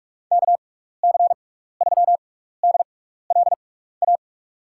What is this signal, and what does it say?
Morse code, heard as a keyed tone, about 46 wpm, spaced as for 15 wpm, 700 Hz. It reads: KC3DLA
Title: USA Call Signs - 6 Characters - 6x Character Spacing - 45wpm